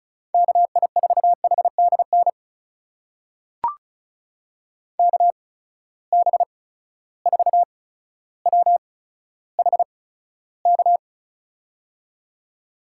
Morse code, heard as a keyed tone, about 36 wpm, spaced as for 15 wpm, 700 Hz. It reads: KI4HDN KB4WHK
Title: USA Call Signs - 6 Characters - 8x Character Spacing - 35wpm